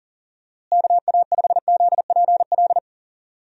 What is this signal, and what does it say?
Morse code, heard as a keyed tone, about 40 wpm, 700 Hz. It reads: KA5ZPL
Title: USA Call Signs - 6 Characters - Repeated 2x - 40wpm